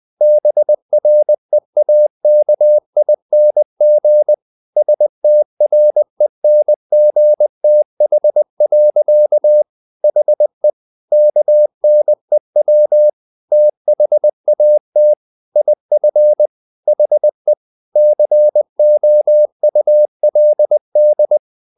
Morse code, heard as a keyed tone, about 20 wpm, 600 Hz. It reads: BREAKING STRENGTH. HE KNEW THAT IF HE COULD
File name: the old man and the sea / the old man and the sea part 1 20wpm